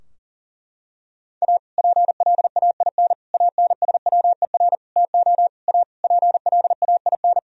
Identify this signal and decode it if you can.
Morse code, heard as a keyed tone, about 40 wpm, 700 Hz. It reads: A PLAIN ANSWER TO A PLAIN
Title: Sentences from Top 1000 Words - Review Part 2 - 40wpm